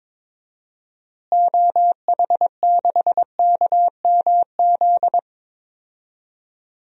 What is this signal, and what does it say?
Morse code, heard as a keyed tone, about 22 wpm, 700 Hz. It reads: OH6KMZ
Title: DXCC-100 Call Signs - 6 Characters - Repeated 3x - 22wpm